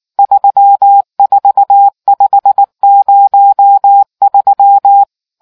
Morse code, heard as a keyed tone, about 19 wpm, 800 Hz. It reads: 34503